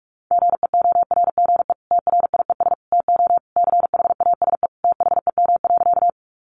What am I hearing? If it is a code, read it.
Morse code, heard as a keyed tone, about 44 wpm, 700 Hz. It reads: GEORGE TRIES TO CHASE THEM.
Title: Additional Easy Phrases 45wpm